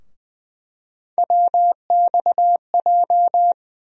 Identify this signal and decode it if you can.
Morse code, heard as a keyed tone, about 21 wpm, 700 Hz. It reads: WXJ
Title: USA Call Sign Postfixes - 3 Letters 20wpm